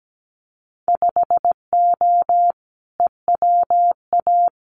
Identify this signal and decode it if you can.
Morse code, heard as a keyed tone, about 17 wpm, 700 Hz. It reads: 5O EWA
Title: Winter Field Day Exchanges Encore 17wpm